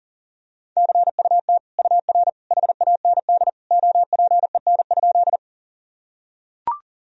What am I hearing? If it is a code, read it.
Morse code, heard as a keyed tone, about 40 wpm, 700 Hz. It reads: CUT UR HAND OPEN?
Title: Sentences from Top 300 Words and Abbreviations 40wpm